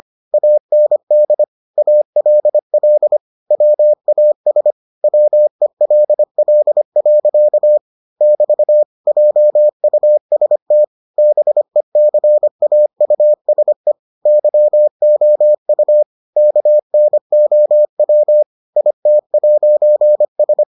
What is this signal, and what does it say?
Morse code, heard as a keyed tone, about 25 wpm, 600 Hz. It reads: AND ALL WAS WELL. <BT> JUST BECAUSE YOU KNOW IT'S